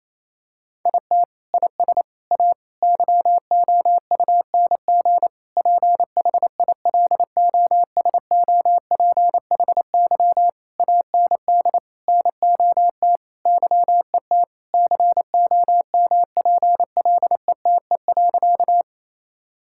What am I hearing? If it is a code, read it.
Morse code, heard as a keyed tone, about 28 wpm, 700 Hz. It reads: IT IS A YOUNG PHILOSOPHY AND NOT YET COMPLETE.
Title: Sentences from Top 1000 Words - Review Part 6 - 28wpm